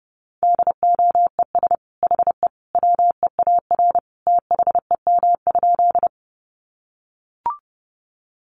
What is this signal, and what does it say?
Morse code, heard as a keyed tone, about 30 wpm, 700 Hz. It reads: DOES HE WEAR THEM?
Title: Sentences from Top 1500 Words 30wpm